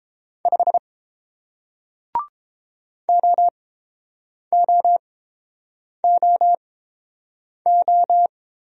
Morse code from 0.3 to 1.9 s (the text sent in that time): 5